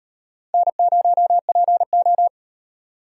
Morse code, heard as a keyed tone, about 38 wpm, 700 Hz. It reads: N0PO